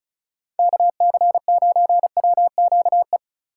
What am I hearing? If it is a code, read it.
Morse code, heard as a keyed tone, about 44 wpm, 700 Hz. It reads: KC9WQE